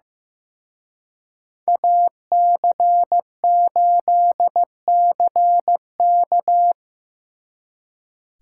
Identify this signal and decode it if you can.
Morse code, heard as a keyed tone, about 18 wpm, 700 Hz. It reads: AC8CK
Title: Speed Racing - USA Call Signs 15wpm